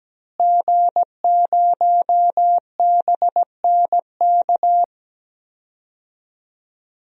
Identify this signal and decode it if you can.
Morse code, heard as a keyed tone, about 17 wpm, 700 Hz. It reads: G0BNK
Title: DXCC-100 Call Signs - 5 Characters - Repeated 2x - 17wpm